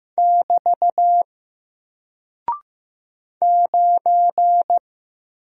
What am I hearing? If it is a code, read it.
Morse code, heard as a keyed tone, about 15 wpm, 700 Hz. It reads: <BT> 9
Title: Single Character (Letter:Number:Punctuation:Prosign) 15wpm